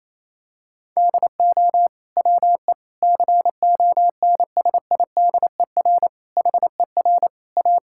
Morse code, heard as a keyed tone, about 28 wpm, 700 Hz. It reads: DO WE CONSIDER HER A
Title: Sentences from Top 1000 Words - Review Part 5 - 28wpm